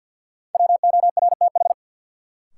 Morse code, heard as a keyed tone, about 51 wpm, 700 Hz. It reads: WORTH
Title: Sentences from Top 800 Words - New Word Drill - 50wpm